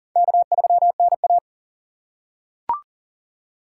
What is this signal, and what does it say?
Morse code, heard as a keyed tone, about 40 wpm, 690 Hz. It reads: K3NA
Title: USA Call Signs - 4x Character Spacing - 40wpm